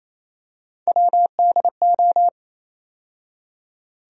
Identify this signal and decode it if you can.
Morse code, heard as a keyed tone, about 28 wpm, 700 Hz. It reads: WDO
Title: USA Call Sign Postfixes - 3 Letters 28wpm